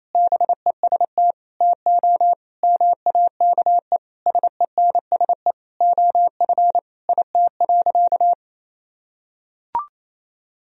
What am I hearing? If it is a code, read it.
Morse code, heard as a keyed tone, about 28 wpm, 700 Hz. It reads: BEST TO MAKE SENSE OF IT.
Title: Sentences from Top 1000 Words - Review Part 4 - 28wpm